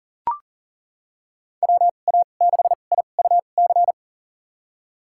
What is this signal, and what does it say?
Morse code, heard as a keyed tone, about 40 wpm, spaced as for 29 wpm, 700 Hz. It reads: WA6IUC
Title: USA Call Signs - 6 Characters - 2x Character Spacing - 40wpm